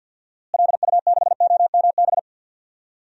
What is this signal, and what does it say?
Morse code, heard as a keyed tone, about 50 wpm, 700 Hz. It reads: LU6OMB
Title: DXCC-100 Call Signs - 6 Characters - Repeated 3x - 50wpm